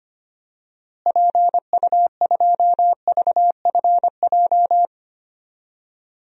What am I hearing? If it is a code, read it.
Morse code, heard as a keyed tone, about 25 wpm, 700 Hz. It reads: PU2VFJ
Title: DXCC-100 Call Signs - 6 Characters - Repeated 2x - 25wpm